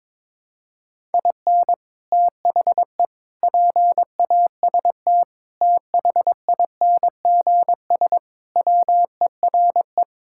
Morse code, heard as a keyed tone, about 22 wpm, 700 Hz. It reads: IN THE PAST THINGS WERE
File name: practice / 22wpm / Sentences with 3-Word Phrases - 22wpm